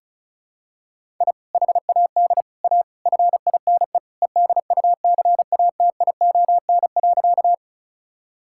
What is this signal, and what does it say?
Morse code, heard as a keyed tone, about 36 wpm, 700 Hz. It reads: I HAD A FINE EDUCATION.
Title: Sentences from Top 1000 Words - Review Part 6 - 35wpm